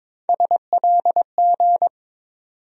Morse code, heard as a keyed tone, about 22 wpm, 700 Hz. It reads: SLG
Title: USA Call Sign Postfixes - 3 Letters 22wpm